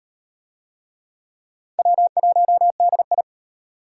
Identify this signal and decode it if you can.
Morse code, heard as a keyed tone, about 31 wpm, 700 Hz. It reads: W1DI
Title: Speed Racing - USA 1x2 Call Signs 25wpm